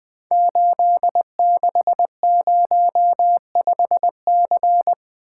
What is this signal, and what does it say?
Morse code, heard as a keyed tone, about 20 wpm, 700 Hz. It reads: 8605C
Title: Code Groups - 5 Letter:Numbers - 20wpm